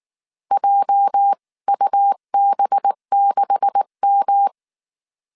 Morse code, heard as a keyed tone, about 19 wpm, 800 Hz. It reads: JUB6M
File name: CW / DET / 1 9 1500C S B 18wpm 2551